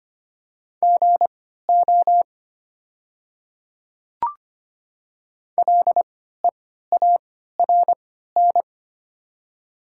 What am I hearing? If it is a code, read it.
Morse code, heard as a keyed tone, about 25 wpm, spaced as for 14 wpm, 700 Hz. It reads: GO LEARN
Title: 3x Character Spacing - Top 300 Words - 25wpm